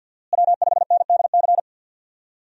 Morse code, heard as a keyed tone, about 51 wpm, 700 Hz. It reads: W5NDC
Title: USA Call Signs - 5 Characters - Repeated 3x - 50wpm